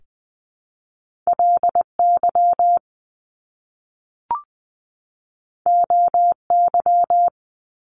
Morse code, heard as a keyed tone, about 20 wpm, 700 Hz. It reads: LY OY
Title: Top Two Letter Combinations in Words 20wpm